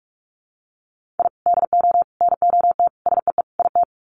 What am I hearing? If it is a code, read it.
Morse code, heard as a keyed tone, about 44 wpm, 700 Hz. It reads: I DO NOT SEE IT
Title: Sets of 5 Words Encore2 45wpm